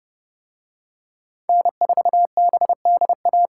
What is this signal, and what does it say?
Morse code, heard as a keyed tone, about 28 wpm, 700 Hz. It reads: N4BDA